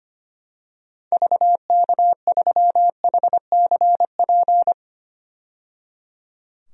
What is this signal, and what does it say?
Morse code, heard as a keyed tone, about 25 wpm, 700 Hz. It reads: VK3HCP
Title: DXCC-100 Call Signs - 6 Characters - 25wpm